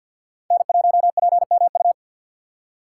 Morse code, heard as a keyed tone, about 52 wpm, 700 Hz. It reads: N1PMU
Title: USA Call Signs - 2x Character Spacing - 50wpm